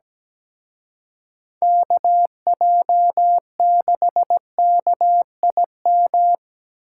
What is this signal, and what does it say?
Morse code, heard as a keyed tone, about 20 wpm, 700 Hz. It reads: KJ6KIM
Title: Speed Racing - USA 2x3 Call Signs 17wpm